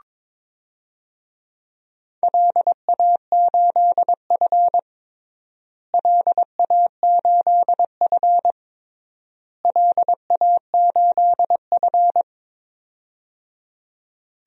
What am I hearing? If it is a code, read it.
Morse code, heard as a keyed tone, about 22 wpm, 700 Hz. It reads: LA8F LA8F LA8F
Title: DXCC-100 Call Signs - 4 Characters - Repeated 3x - 22wpm